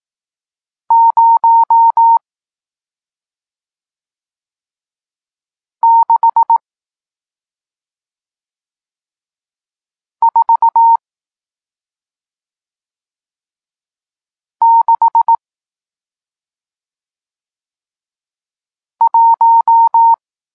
Morse code, heard as a keyed tone, about 18 wpm, 920 Hz. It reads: 0 6 4 6 1
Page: Tekst 40 – Nauka znaków przestankowych